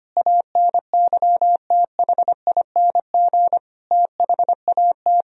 Morse code, heard as a keyed tone, about 25 wpm, 700 Hz. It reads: ANYTHING THAT
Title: Sentences from Top 1000 Words - Review Part 1 - 25wpm